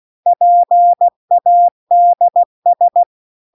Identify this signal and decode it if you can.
Morse code, heard as a keyed tone, about 16 wpm, 700 Hz. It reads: PADS